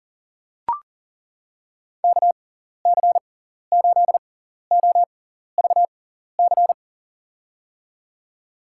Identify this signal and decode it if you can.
Morse code, heard as a keyed tone, about 37 wpm, spaced as for 14 wpm, 700 Hz. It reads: KC8OVC
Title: USA Call Signs - 6x Character Spacing - 40wpm